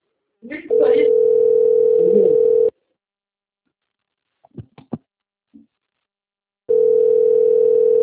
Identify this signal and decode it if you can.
Ringback tone (North America), 440 + 480 Hz, with speech.
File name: rg-1600-unknown- - - .35490